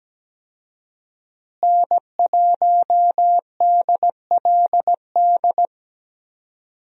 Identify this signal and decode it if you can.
Morse code, heard as a keyed tone, about 17 wpm, 700 Hz. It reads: N1DLD
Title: USA Call Signs - 5 Characters - Repeated 2x - 17wpm